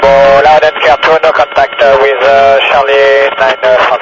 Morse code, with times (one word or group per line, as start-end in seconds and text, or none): none